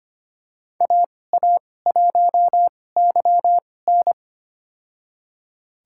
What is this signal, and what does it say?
Morse code, heard as a keyed tone, about 25 wpm, spaced as for 18 wpm, 700 Hz. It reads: AA1YN
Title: USA Call Signs - 5 Characters - 2x Character Spacing - 25wpm